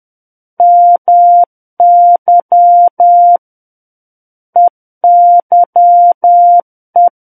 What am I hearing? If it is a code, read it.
Morse code, heard as a keyed tone, about 10 wpm, 700 Hz. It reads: MY EYE
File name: Book One / 01-Chapter 1 - 10 WPM